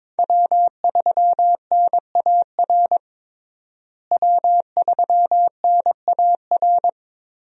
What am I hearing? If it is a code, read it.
Morse code, heard as a keyed tone, about 22 wpm, 700 Hz. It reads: W3NAR W3NAR
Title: USA Call Signs - 5 Characters - Repeated 3x - 22wpm